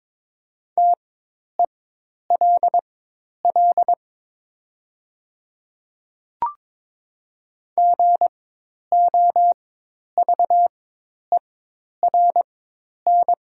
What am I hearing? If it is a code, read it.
Morse code, heard as a keyed tone, about 22 wpm, spaced as for 10 wpm, 690 Hz. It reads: TELL GOVERN
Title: 4x Character Spacing - Top 500 Words - 22wpm